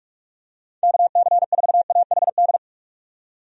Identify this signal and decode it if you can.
Morse code, heard as a keyed tone, about 48 wpm, 700 Hz. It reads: KC4AHD
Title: USA Call Signs - 6 Characters - 45wpm